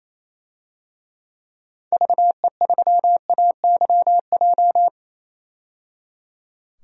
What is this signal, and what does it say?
Morse code, heard as a keyed tone, about 28 wpm, 700 Hz. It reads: VE3AYJ
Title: DXCC-100 Call Signs - 6 Characters - 28wpm